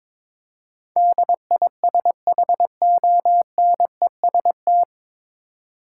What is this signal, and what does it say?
Morse code, heard as a keyed tone, about 22 wpm, 700 Hz. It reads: DISHONEST
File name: 9 Letter Words - First 3 Letters Given - 22wpm